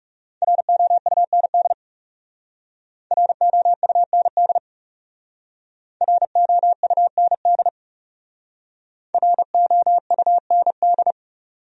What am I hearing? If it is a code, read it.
Morse code, heard as a keyed tone, about 38 wpm, 700 Hz. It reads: ROUND ROUND ROUND ROUND